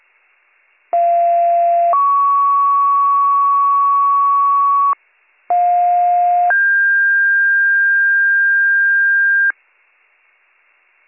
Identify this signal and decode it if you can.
Two-tone page, heard as a one-second tone, then a three-second tone, about 690 then 1120 Hz, about 710 then 1600 Hz. It is heard from 0.9 to 9.5 s.